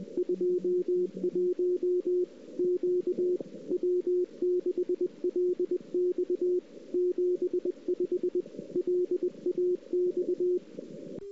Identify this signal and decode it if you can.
Morse code, heard as a keyed tone, about 20 wpm, 360 Hz. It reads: 21 Q W6LX 75 LAX
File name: CW / 2008-ss-cw-k5tr-op-n5rz -05:00